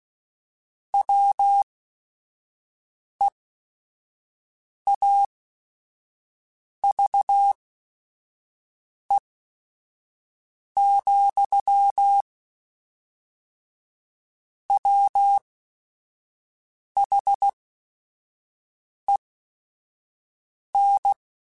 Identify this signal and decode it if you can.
Morse code, heard as a keyed tone, about 16 wpm, spaced as for 5 wpm, 800 Hz. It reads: WEAVE, WHEN
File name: Quotes / Quotes-mp3s / Deceive @5